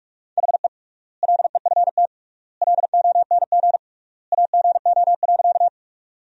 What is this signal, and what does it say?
Morse code, heard as a keyed tone, about 44 wpm, 700 Hz. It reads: HE LEFT LONG AGO.
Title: 3x Word-Spacing - Sentences from Top 500 Words - 45wpm